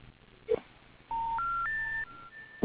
Special information tone.